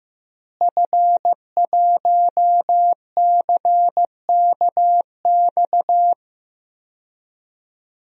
Morse code, heard as a keyed tone, about 15 wpm, 700 Hz. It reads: F1CKX
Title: DXCC-100 Call Signs - 5 Characters - Repeated 2x - 15wpm